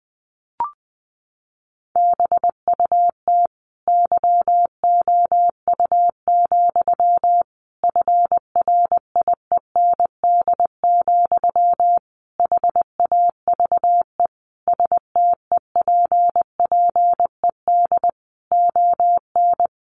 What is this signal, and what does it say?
Morse code, heard as a keyed tone, about 19 wpm, 700 Hz. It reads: BUT YOU, FRIEND, HAVE STEPPED ON
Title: A Sound Of Thunder 20wpm